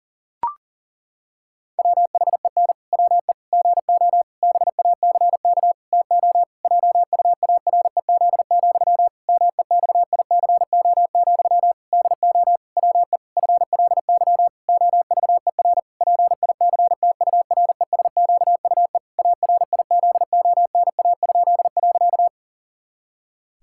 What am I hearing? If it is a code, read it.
Morse code, heard as a keyed tone, about 40 wpm, 700 Hz. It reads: WHEN WE GO BACK TO JUAREZ, MEXICO, DO WE FLY OVER PICTURESQUE ARIZONA?.
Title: Pangram Challenge v2 - 40wpm